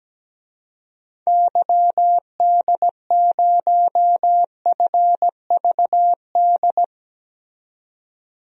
Morse code, heard as a keyed tone, about 17 wpm, 700 Hz. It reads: YD0FVD
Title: DXCC-100 Call Signs - 6 Characters - 17wpm